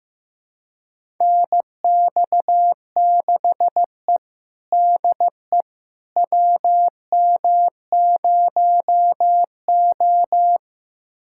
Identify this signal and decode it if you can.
Morse code, heard as a keyed tone, about 15 wpm, 700 Hz. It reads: NX6E DE WM0O
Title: USA Callsign DE Callsign - 4 Characters - 15wpm